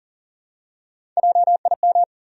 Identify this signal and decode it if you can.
Morse code, heard as a keyed tone, about 40 wpm, 700 Hz. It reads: JIM
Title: USA Common Names - Max 4 Letters - 40wpm